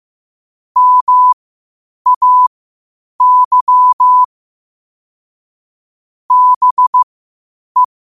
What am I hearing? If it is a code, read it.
Morse code, heard as a keyed tone, about 15 wpm, spaced as for 8 wpm, 1000 Hz. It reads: MAY BE
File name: qotdfiles / 8wpm